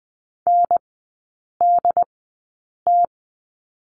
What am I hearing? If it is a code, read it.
Morse code, heard as a keyed tone, about 20 wpm, spaced as for 8 wpm, 690 Hz. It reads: NDT
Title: 081 Lesson 14e - Number 2 - Three Characters 20wpm